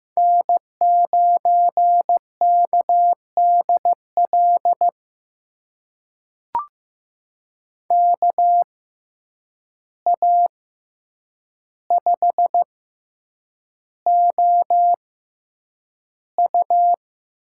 Morse code, heard as a keyed tone, about 15 wpm, spaced as for 8 wpm, 700 Hz. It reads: N9KDL KA5OU
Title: USA Call Signs - 6x Character Spacing - 15wpm